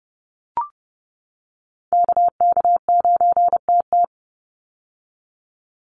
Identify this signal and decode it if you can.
Morse code, heard as a keyed tone, about 30 wpm, 700 Hz. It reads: KK9TT
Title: USA Call Signs 30wpm